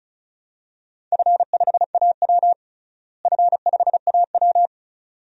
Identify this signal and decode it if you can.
Morse code, heard as a keyed tone, about 34 wpm, 700 Hz. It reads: F5AW F5AW
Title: DXCC-100 Call Signs - 4 Characters - Repeated 2x - 35wpm